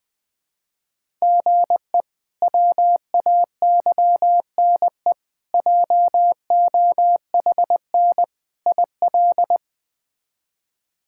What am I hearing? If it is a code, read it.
Morse code, heard as a keyed tone, about 20 wpm, 700 Hz. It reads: GE WAYNE JOHN IL
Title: K1USN SST QSO Practice - 20wpm